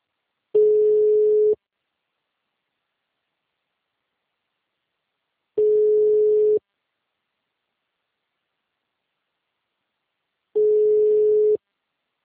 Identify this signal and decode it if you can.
Ringback tone (Europe, 425 Hz).